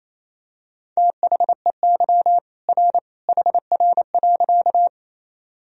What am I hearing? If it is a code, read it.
Morse code, heard as a keyed tone, about 28 wpm, 700 Hz. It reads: THEY R HR.
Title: Sentences from Top 200 Words and Abbreviations 28wpm